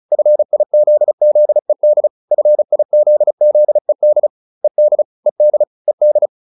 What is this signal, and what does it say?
Morse code, heard as a keyed tone, about 34 wpm, 600 Hz. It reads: FIZZED FIZZED ED ED ED